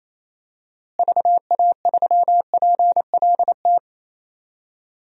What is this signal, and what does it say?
Morse code, heard as a keyed tone, about 28 wpm, 700 Hz. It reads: VA3PLT
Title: DXCC-100 Call Signs - 6 Characters - Repeated 3x - 28wpm